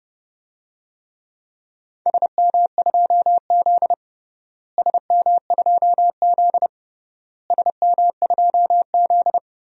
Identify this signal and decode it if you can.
Morse code, heard as a keyed tone, about 30 wpm, 700 Hz. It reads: SM2Z SM2Z SM2Z
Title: DXCC-100 Call Signs - 4 Characters - Repeated 3x - 30wpm